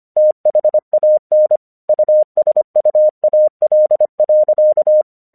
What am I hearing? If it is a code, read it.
Morse code, heard as a keyed tone, about 25 wpm, 610 Hz. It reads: THAN USUAL.
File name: English 25wpm